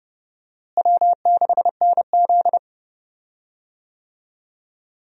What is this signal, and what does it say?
Morse code, heard as a keyed tone, about 29 wpm, 700 Hz. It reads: W6NZ
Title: USA Call Signs - 4 Characters - Repeated 2x - 30wpm